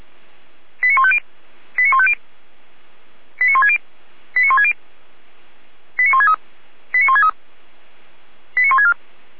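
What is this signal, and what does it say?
Five-tone selective call (ZVEI), heard as quick runs of five tones, tones of about 70 ms. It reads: 89150 89150 89160 89160 89163 89163 89164
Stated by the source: ZVEI five-tone selective call received by SDR.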